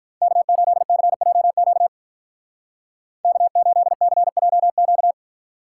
Morse code, heard as a keyed tone, about 43 wpm, 700 Hz. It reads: K8CJQ K8CJQ
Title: Speed Racing - USA Call Signs 35wpm